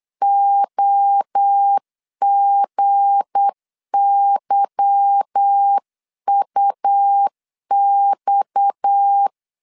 Morse code, heard as a keyed tone, about 8 wpm, 800 Hz. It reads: OGYUX
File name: CW / DET / 1 9 1500C S B 8wpm 5624